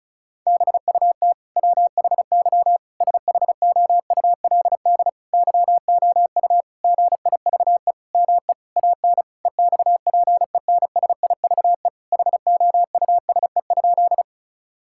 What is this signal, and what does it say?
Morse code, heard as a keyed tone, about 36 wpm, 700 Hz. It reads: BUT WHY SHOULD YOU GIVE ME AN EXPENSIVE HOUSE?
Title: Sentences from Top 1000 Words - Review Part 6 - 35wpm